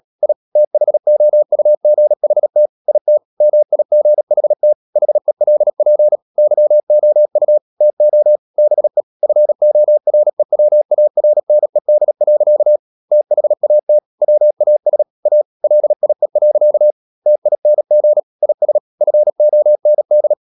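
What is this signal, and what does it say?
Morse code, heard as a keyed tone, about 37 wpm, 600 Hz. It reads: I THOUGHT IT MIGHT HELP YOU TO BE FOREWARNED. THAT WAS A LIE. TING IS FOND
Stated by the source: Morse code practice at 35–39 wpm sending English text of Ken Follett's novel Never, from CW med Gnister.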